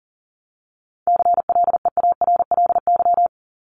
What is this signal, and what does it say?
Morse code, heard as a keyed tone, about 40 wpm, 700 Hz. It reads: CLEARLY
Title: LY Suffix - Top 100 Words with Base Words 40wpm